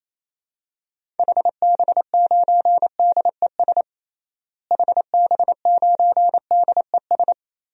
Morse code, heard as a keyed tone, about 28 wpm, 700 Hz. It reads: HB9DES HB9DES
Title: DXCC-100 Call Signs - 6 Characters - Repeated 2x - 28wpm